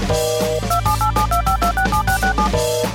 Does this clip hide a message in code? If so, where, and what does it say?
Yes: DTMF from 0.7 to 2.5 s, heard as telephone touch tones, tones of about 110 ms. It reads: 278725267667